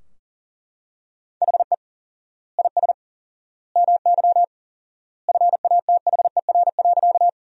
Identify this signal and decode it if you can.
Morse code, heard as a keyed tone, about 40 wpm, 700 Hz. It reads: HE IS MY FATHER.
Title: 4x Word-Spacing - Sentences from Top 300 Words - 40wpm